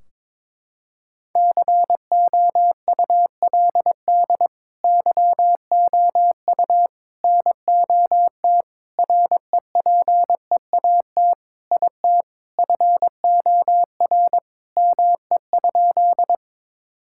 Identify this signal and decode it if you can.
Morse code, heard as a keyed tone, about 22 wpm, 700 Hz. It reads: COULD YOU NOT REPEAT IT FOR ME?
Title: Sentences from Top 1600 Words 22wpm